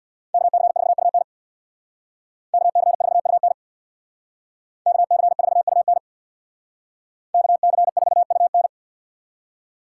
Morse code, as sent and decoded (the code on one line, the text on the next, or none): -.- -.-. ....- ..- -. / -.- -.-. ....- ..- -. / -.- -.-. ....- ..- -. / -.- -.-. ....- ..- -.
KC4UN KC4UN KC4UN KC4UN